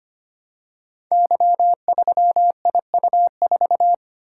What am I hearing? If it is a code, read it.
Morse code, heard as a keyed tone, about 25 wpm, 700 Hz. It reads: Y3IU4
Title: Code Groups - 5 Letter:Numbers - 25wpm